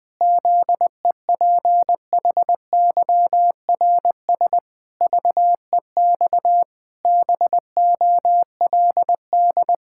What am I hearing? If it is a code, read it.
Morse code, heard as a keyed tone, about 21 wpm, 700 Hz. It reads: ZEPHYRS VEX BOLD
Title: Pangram Challenge v2 - 20wpm